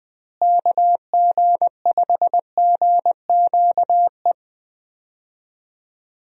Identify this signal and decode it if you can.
Morse code, heard as a keyed tone, about 20 wpm, 700 Hz. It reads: KG5GQE
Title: USA Call Signs - 6 Characters - Repeated 3x - 20wpm